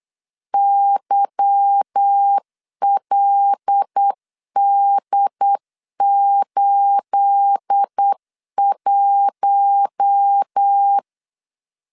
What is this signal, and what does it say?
Morse code, heard as a keyed tone, about 8 wpm, 800 Hz. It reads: YLD81
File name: LEZIONI / 9 28 8wpm 3512